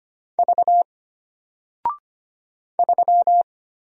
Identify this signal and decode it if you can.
Morse code, heard as a keyed tone, about 25 wpm, 700 Hz. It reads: V 3
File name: Single-Letter-Number-25wpm